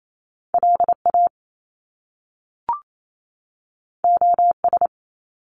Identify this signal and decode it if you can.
Morse code, heard as a keyed tone, about 28 wpm, 700 Hz. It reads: LA OS
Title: Top Two Letter Combinations in Words 28wpm